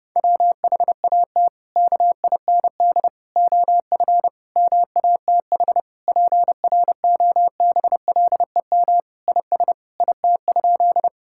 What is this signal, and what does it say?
Morse code, heard as a keyed tone, about 32 wpm, 700 Hz. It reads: WHAT KIND OF MATH PROBLEM IS IT?
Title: Sentences from Top 1000 Words - Review Part 6 - 30wpm